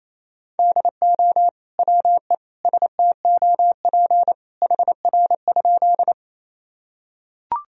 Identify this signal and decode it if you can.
Morse code, heard as a keyed tone, about 28 wpm, 700 Hz. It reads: DO WE STOP HR?
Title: Sentences from Top 300 Words and Abbreviations 28wpm